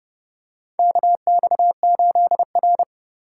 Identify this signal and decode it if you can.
Morse code, heard as a keyed tone, about 30 wpm, 700 Hz. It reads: KX8R
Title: USA Call Signs - 4 Characters - 2x Character Spacing - 30wpm